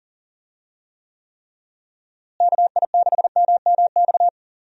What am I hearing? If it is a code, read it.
Morse code, heard as a keyed tone, about 40 wpm, 700 Hz. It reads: KI6MMX
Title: USA Call Signs - 6 Characters - Repeated 3x - 40wpm